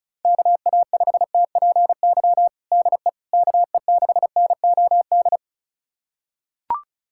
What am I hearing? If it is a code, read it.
Morse code, heard as a keyed tone, about 34 wpm, 700 Hz. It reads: KA5TPY DE KE6NOD
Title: USA Callsign DE Callsign - 6 Characters - 35wpm